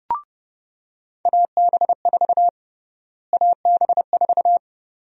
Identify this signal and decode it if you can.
Morse code, heard as a keyed tone, about 29 wpm, 700 Hz. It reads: AB4 AB4
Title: USA Call Sign Prefixes - Repeated 2x - 30wpm